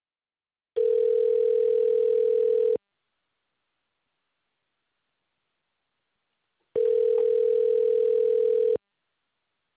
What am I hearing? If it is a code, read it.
Ringback tone (North America), 440 + 480 Hz.